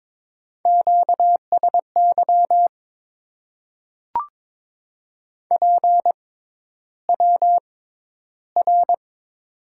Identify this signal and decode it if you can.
Morse code, heard as a keyed tone, about 22 wpm, spaced as for 11 wpm, 700 Hz. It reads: QSY PWR
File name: Top 90 QSO Elements - 6x Character Spacing - 22wpm